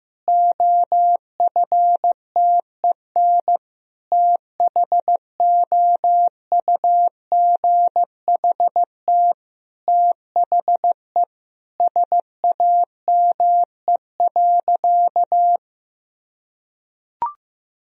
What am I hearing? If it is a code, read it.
Morse code, heard as a keyed tone, about 16 wpm, 700 Hz. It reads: OFTEN THOUGHT THE SAME.
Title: Sentences from Top 1000 Words - Review Part 1 - 15wpm